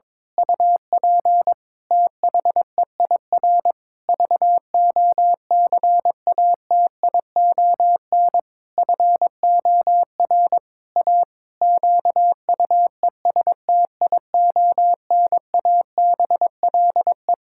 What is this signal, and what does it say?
Morse code, heard as a keyed tone, about 22 wpm, 700 Hz. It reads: UP THEIR VOCATION FOR A QUESTIONABLE
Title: Pangram Challenge v2 - 22wpm